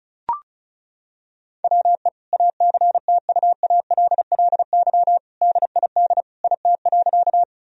Morse code, heard as a keyed tone, about 34 wpm, 700 Hz. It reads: WE ACTUALLY DID IT.
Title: Sentences from Top 1000 Words - Review Part 4 - 35wpm